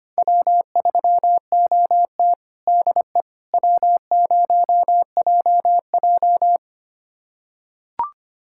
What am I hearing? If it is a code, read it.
Morse code, heard as a keyed tone, about 25 wpm, 700 Hz. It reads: W3OT DE W0JJ
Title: USA Callsign DE Callsign - 4 Characters - 25wpm